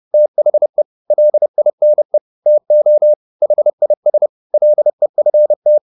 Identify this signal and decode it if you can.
Morse code, heard as a keyed tone, about 30 wpm, 600 Hz. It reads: THE LINE TO HIS LEFT
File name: the old man and the sea / the old man and the sea 1 30wpm